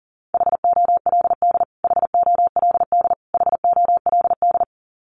Morse code, heard as a keyed tone, about 40 wpm, 700 Hz. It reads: HOLD HOLD HOLD
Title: Top 500 Words Repeated 40wpm